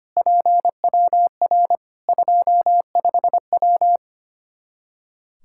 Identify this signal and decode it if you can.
Morse code, heard as a keyed tone, about 25 wpm, 700 Hz. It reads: PWR 25W
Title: K1USN SST QSO Practice 25wpm